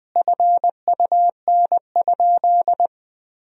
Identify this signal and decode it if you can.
Morse code, heard as a keyed tone, about 20 wpm, 700 Hz. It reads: FUN?
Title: Sentences from Top 1000 Words - Review Part 3 - 20wpm